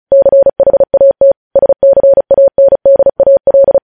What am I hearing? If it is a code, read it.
Morse code, heard as a keyed tone, about 36 wpm, 550 Hz. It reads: CHAT SCANDAL
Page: News headlines in Morse code at 35 wpm